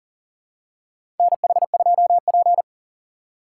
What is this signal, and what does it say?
Morse code, heard as a keyed tone, about 38 wpm, 700 Hz. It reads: NH2P
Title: USA Call Signs - 4 Characters - 6x Character Spacing - 40wpm